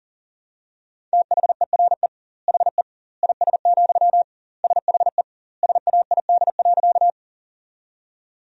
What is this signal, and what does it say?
Morse code, heard as a keyed tone, about 40 wpm, 700 Hz. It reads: THERE HE IS, SHE SAID.
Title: 2x Word-Spacing - Sentences from Top 100 Words - 40wpm